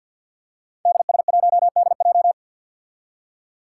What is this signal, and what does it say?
Morse code, heard as a keyed tone, about 51 wpm, 700 Hz. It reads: DS1DJ
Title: Five Challenging Letter-Number Combinations 50wpm